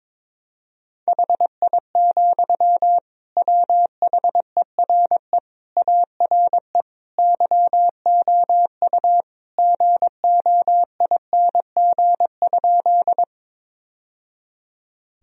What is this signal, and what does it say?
Morse code, heard as a keyed tone, about 22 wpm, 700 Hz. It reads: HI, WHERE ARE YOU GOING?
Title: Sentences from Top 1000 Words - Review Part 3 - 22wpm